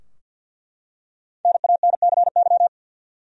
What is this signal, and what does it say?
Morse code, heard as a keyed tone, about 49 wpm, 700 Hz. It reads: NANCY
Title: USA Common Names - 50wpm